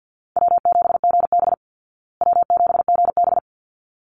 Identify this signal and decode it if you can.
Morse code, heard as a keyed tone, about 50 wpm, 700 Hz. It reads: W7GB W7GB
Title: Easy Call Signs Repeated 2x - 50wpm